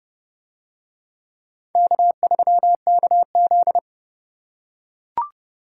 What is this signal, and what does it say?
Morse code, heard as a keyed tone, about 30 wpm, 700 Hz. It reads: K3KZ